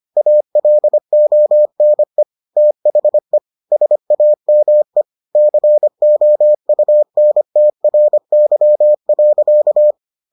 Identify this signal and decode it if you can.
Morse code, heard as a keyed tone, about 25 wpm, 600 Hz. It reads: ALONE THE SAME COUNTRY.